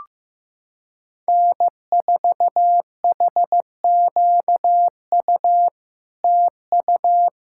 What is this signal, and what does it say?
Morse code, heard as a keyed tone, about 15 wpm, 700 Hz. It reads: N4HQU TU
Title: SOTA QSO Practice 15wpm